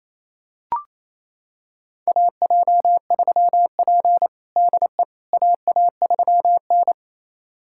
Morse code, heard as a keyed tone, about 28 wpm, 700 Hz. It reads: AJ3P DE AA3N
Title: USA Callsign DE Callsign - 4 Characters - 28wpm